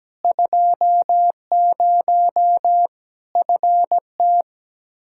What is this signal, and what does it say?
Morse code, heard as a keyed tone, about 17 wpm, 700 Hz. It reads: 20 FT
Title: K1USN SST QSO Practice 17wpm